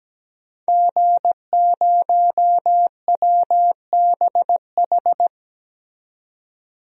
Morse code, heard as a keyed tone, about 17 wpm, 700 Hz. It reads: G0WBH
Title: DXCC-100 Call Signs - 5 Characters - 17wpm